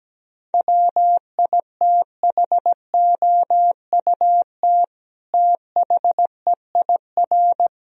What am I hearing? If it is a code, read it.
Morse code, heard as a keyed tone, about 17 wpm, 700 Hz. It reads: WITHOUT THEIR
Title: Sentences from Top 1000 Words - Review Part 3 - 17wpm